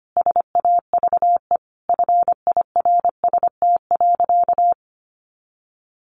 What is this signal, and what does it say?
Morse code, heard as a keyed tone, about 25 wpm, 700 Hz. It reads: SAVE FIRST.
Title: Sentences from Top 1100 Words 25wpm